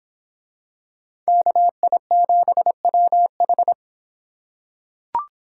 Morse code, heard as a keyed tone, about 25 wpm, 700 Hz. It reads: KI7WH